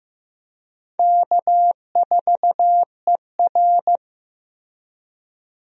Morse code, heard as a keyed tone, about 15 wpm, 700 Hz. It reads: K4ER
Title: USA Call Signs - 4 Characters - 15wpm